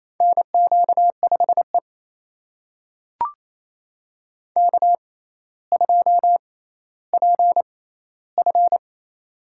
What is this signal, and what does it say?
Morse code, heard as a keyed tone, about 28 wpm, spaced as for 14 wpm, 700 Hz. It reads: NQ5E K2PF
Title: USA Call Signs - 4 Characters - 6x Character Spacing - 28wpm